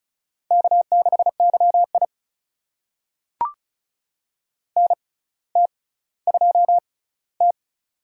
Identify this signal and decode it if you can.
Morse code, heard as a keyed tone, about 36 wpm, spaced as for 18 wpm, 700 Hz. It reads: K6YI NT2T
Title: USA Call Signs - 4 Characters - 6x Character Spacing - 35wpm